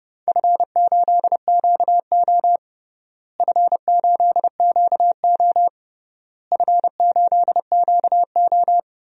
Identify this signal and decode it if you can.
Morse code, heard as a keyed tone, about 31 wpm, 700 Hz. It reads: F8QO F8QO F8QO
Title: DXCC-100 Call Signs - 4 Characters - Repeated 3x - 30wpm